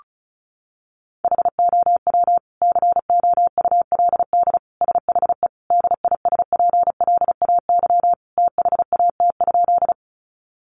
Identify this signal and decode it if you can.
Morse code, heard as a keyed tone, about 34 wpm, 700 Hz. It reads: HOW COULD SHE DISPLAY THAT?
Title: Sentences from Top 1200 Words 35wpm